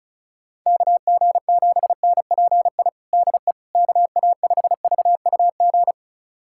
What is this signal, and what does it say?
Morse code, heard as a keyed tone, about 36 wpm, 700 Hz. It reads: KG7NPI DE KA5VUG
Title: USA Callsign DE Callsign - 6 Characters - 35wpm